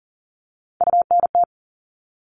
Morse code, heard as a keyed tone, about 40 wpm, 700 Hz. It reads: UNT
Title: All Cut Numbers - 3-Digit 40wpm